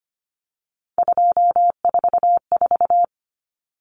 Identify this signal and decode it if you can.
Morse code, heard as a keyed tone, about 25 wpm, 690 Hz. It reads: 244
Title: Numbers - 3-Digit 25wpm